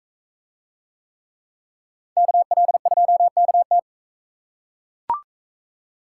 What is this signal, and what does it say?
Morse code, heard as a keyed tone, about 43 wpm, 700 Hz. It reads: KL2KT